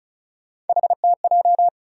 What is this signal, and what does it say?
Morse code, heard as a keyed tone, about 34 wpm, 700 Hz. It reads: HTJ
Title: USA Call Sign Postfixes - 3 Letters 35wpm